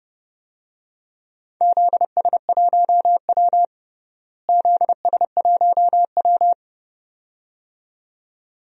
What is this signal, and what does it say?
Morse code, heard as a keyed tone, about 32 wpm, 700 Hz. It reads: ZS1W ZS1W
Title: DXCC-100 Call Signs - 4 Characters - Repeated 2x - 30wpm